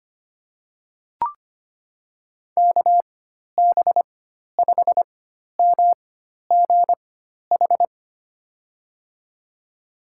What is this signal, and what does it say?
Morse code, heard as a keyed tone, about 25 wpm, spaced as for 12 wpm, 700 Hz. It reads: KB5MGH
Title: USA Call Signs - 6 Characters - 4x Character Spacing - 25wpm